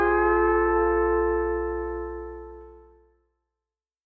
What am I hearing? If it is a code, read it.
Dial tone (North America), 350 + 440 Hz.